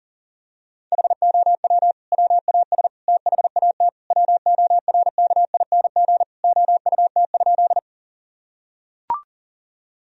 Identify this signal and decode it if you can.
Morse code, heard as a keyed tone, about 40 wpm, 700 Hz. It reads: HOW WAS THAT WORKING OUT?
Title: Sentences from Top 1000 Words - Review Part 4 - 40wpm